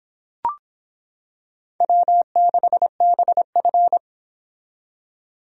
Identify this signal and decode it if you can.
Morse code, heard as a keyed tone, about 26 wpm, 700 Hz. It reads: W6BF